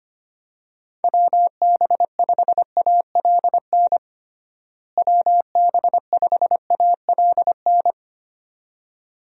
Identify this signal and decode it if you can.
Morse code, heard as a keyed tone, about 25 wpm, 700 Hz. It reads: WB5ALN WB5ALN
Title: USA Call Signs - 6 Characters - Repeated 2x - 25wpm